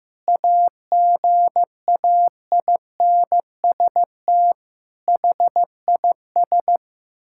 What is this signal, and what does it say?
Morse code, heard as a keyed tone, about 16 wpm, 700 Hz. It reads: AGAINST HIS
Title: Sentences from Top 1600 Words 15wpm